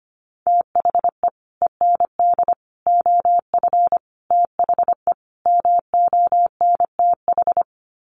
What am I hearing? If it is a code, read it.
Morse code, heard as a keyed tone, about 25 wpm, 700 Hz. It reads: THE END OF THE MONTH
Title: Sets of 5 Words Encore 25wpm